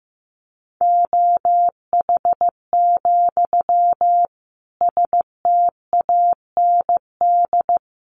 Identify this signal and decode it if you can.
Morse code, heard as a keyed tone, about 15 wpm, 700 Hz. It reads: OH, STAND
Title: Sentences from Top 1400 Words 15wpm